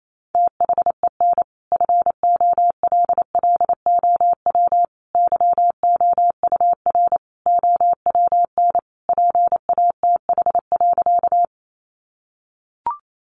Morse code, heard as a keyed tone, about 28 wpm, 700 Hz. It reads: THEN FOLLOW YOUR OWN PATH.
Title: Sentences from Top 1400 Words 28wpm